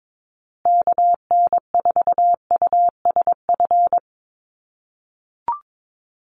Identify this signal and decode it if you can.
Morse code, heard as a keyed tone, about 22 wpm, 700 Hz. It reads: KN4USF
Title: USA Call Signs Repeated 2x 22wpm